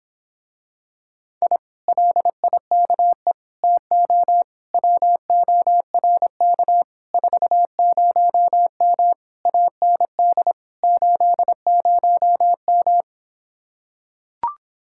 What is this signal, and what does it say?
Morse code, heard as a keyed tone, about 26 wpm, 700 Hz. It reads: I LIKE TO WORK 40M AND 80M